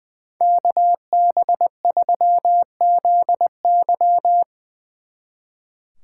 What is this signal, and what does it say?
Morse code, heard as a keyed tone, about 22 wpm, 700 Hz. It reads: KB3ZY
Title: Speed Racing - USA 2x2 Call Signs 20wpm